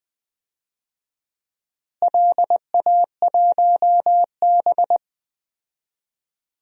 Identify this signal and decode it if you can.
Morse code, heard as a keyed tone, about 21 wpm, 700 Hz. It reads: LA1B
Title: DXCC-100 Call Signs - 4 Characters - Repeated 2x - 20wpm